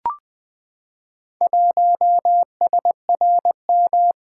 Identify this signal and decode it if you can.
Morse code, heard as a keyed tone, about 20 wpm, 700 Hz. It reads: 1SRM
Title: Four Letter-Number Combinations - 20wpm